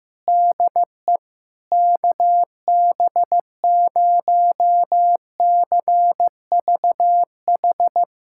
Morse code, heard as a keyed tone, about 15 wpm, 700 Hz. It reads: DE KB0CVH
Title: USA Callsign DE Callsign 15wpm